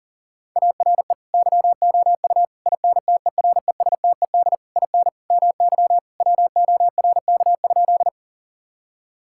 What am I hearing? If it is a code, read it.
Morse code, heard as a keyed tone, about 40 wpm, 700 Hz. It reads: ARE YOU INTERESTED IN MY WORK?
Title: Sentences from Top 1000 Words - Review Part 6 - 40wpm